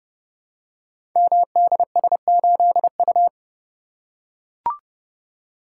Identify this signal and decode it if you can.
Morse code, heard as a keyed tone, about 30 wpm, 700 Hz. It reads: MDS8U
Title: Code Groups - 5 Letter:Numbers - 30wpm